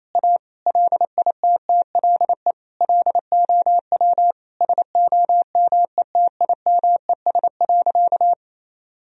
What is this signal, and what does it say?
Morse code, heard as a keyed tone, about 28 wpm, 700 Hz. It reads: A LITTLE LOW SOMETIMES.
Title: Sentences from Top 1000 Words - Review Part 3 - 28wpm